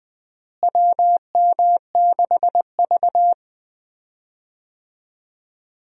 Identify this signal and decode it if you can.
Morse code, heard as a keyed tone, about 21 wpm, 700 Hz. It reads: WM6V
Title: USA Call Signs - 4 Characters - Repeated 2x - 20wpm